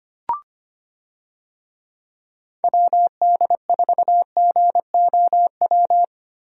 Morse code, heard as a keyed tone, about 25 wpm, 700 Hz. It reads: WD4GOW
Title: USA Call Signs - 6 Characters - 25wpm